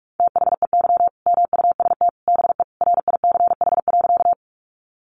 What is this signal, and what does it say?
Morse code, heard as a keyed tone, about 44 wpm, 700 Hz. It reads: THEY MUST BE RICH.
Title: Sentences from Top 1100 Words 45wpm